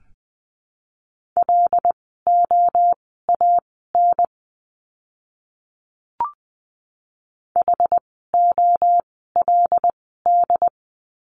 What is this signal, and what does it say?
Morse code, heard as a keyed tone, about 20 wpm, spaced as for 14 wpm, 700 Hz. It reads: LOAN HOLD
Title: 075 Lesson 13f - Letter L - Words 20wpm